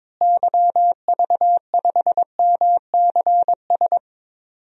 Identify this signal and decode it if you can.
Morse code, heard as a keyed tone, about 21 wpm, 700 Hz. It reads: YV5MCS
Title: DXCC-100 Call Signs - 6 Characters - Repeated 2x - 22wpm